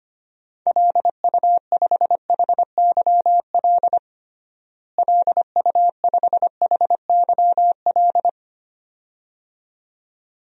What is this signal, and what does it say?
Morse code, heard as a keyed tone, about 25 wpm, 700 Hz. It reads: LU5HYL LU5HYL
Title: DXCC-100 Call Signs - 6 Characters - Repeated 2x - 25wpm